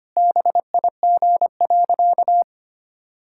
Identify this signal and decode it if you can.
Morse code, heard as a keyed tone, about 25 wpm, 700 Hz. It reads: BIG.
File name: practice / 25wpm / Sentences with 2-Word Phrases - 25wpm